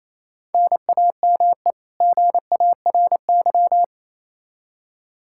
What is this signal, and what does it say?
Morse code, heard as a keyed tone, about 28 wpm, 700 Hz. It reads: NAME GARY
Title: Practice QSO - Exchange 1 - Beginning - 28wpm